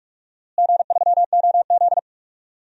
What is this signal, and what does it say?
Morse code, heard as a keyed tone, about 42 wpm, 700 Hz. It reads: G3OZ
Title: DXCC-100 Call Signs - 4 Characters - Repeated 2x - 45wpm